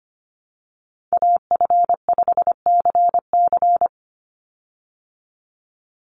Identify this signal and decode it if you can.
Morse code, heard as a keyed tone, about 25 wpm, 700 Hz. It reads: AF5CC
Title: Call Signs - 25wpm